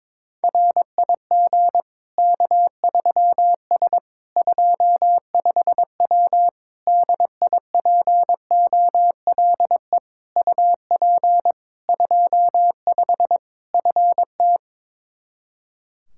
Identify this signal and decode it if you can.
Morse code, heard as a keyed tone, about 22 wpm, 700 Hz. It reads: RIG K3S 25W DIPOLE UP 25 FT
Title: K1USN SST QSO Practice 22wpm